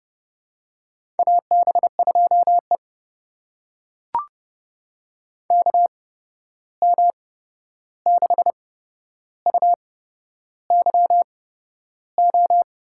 Morse code, heard as a keyed tone, about 30 wpm, spaced as for 11 wpm, 700 Hz. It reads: AB2E KM6UYO
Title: USA Call Signs - 8x Character Spacing - 30wpm